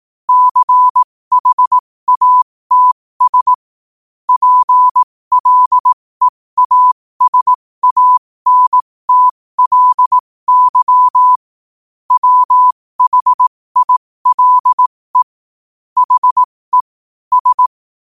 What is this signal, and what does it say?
Morse code, heard as a keyed tone, about 18 wpm, spaced as for 15 wpm, 1000 Hz. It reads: CHATS PLEASANTLY WHILE HE S
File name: qotdfiles / 15wpm